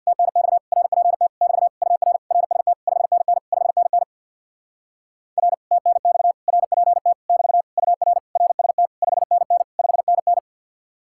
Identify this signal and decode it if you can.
Morse code, heard as a keyed tone, about 60 wpm, 700 Hz. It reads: TNX RPT <BT> UR RST 5NN 5NN R TNX RPT <BT> UR RST 5NN 5NN
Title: Speed-Racing Common QSO Phrases - 50wpm